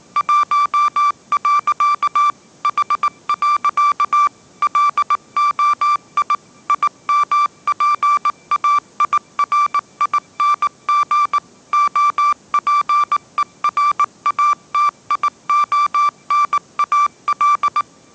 Morse code, heard as a keyed tone, about 19 wpm, 1170 Hz. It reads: 1. H. LOI IMPAIRING OPERATIONAL